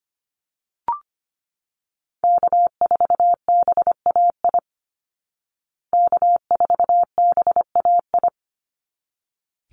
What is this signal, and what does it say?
Morse code, heard as a keyed tone, about 25 wpm, 700 Hz. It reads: K4BAI K4BAI
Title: Call Signs Repeated 3x - 25wpm